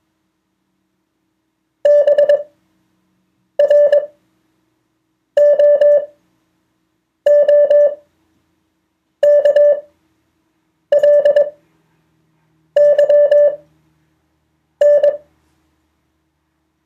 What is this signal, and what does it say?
Morse code, heard as a keyed tone, about 21 wpm, spaced as for 6 wpm, 580 Hz. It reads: BROOKLYN